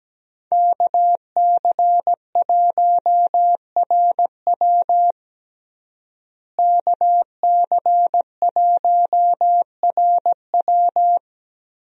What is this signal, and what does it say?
Morse code, heard as a keyed tone, about 17 wpm, 700 Hz. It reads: KC1RW KC1RW
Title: USA Call Signs - 5 Characters - Repeated 2x - 17wpm